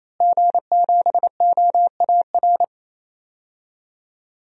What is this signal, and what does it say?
Morse code, heard as a keyed tone, about 28 wpm, 700 Hz. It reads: G7OAR
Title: DXCC-100 Call Signs - 5 Characters - Repeated 3x - 28wpm